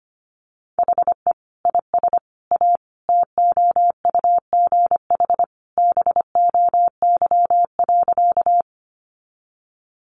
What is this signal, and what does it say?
Morse code, heard as a keyed tone, about 25 wpm, 700 Hz. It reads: HE IS A TOUGH BOY.
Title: Sentences from Top 1200 Words 25wpm